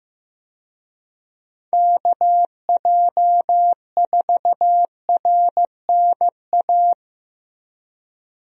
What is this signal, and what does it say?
Morse code, heard as a keyed tone, about 16 wpm, 700 Hz. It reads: KJ4RNA
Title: USA Call Signs - 6 Characters - Repeated 2x - 15wpm